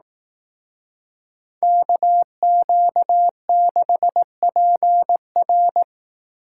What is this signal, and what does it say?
Morse code, heard as a keyed tone, about 18 wpm, 700 Hz. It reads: KQ6PR
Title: Speed Racing - USA 2x2 Call Signs 15wpm